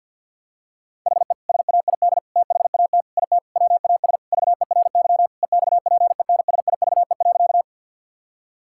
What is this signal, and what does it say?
Morse code, heard as a keyed tone, about 50 wpm, 700 Hz. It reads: HE SAID THAT IT WAS VERY EXPENSIVE.
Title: Sentences from Top 1000 Words - Review Part 6 - 50wpm